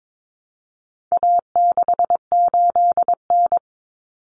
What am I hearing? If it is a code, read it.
Morse code, heard as a keyed tone, about 22 wpm, 700 Hz. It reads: A68N
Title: Common Cut Numbers - 4-Digit 22wpm